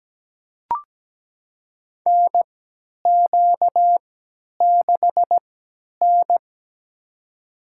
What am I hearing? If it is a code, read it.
Morse code, heard as a keyed tone, about 17 wpm, spaced as for 10 wpm, 700 Hz. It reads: NQ6N
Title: USA Call Signs - 4 Characters - 3x Character Spacing - 17wpm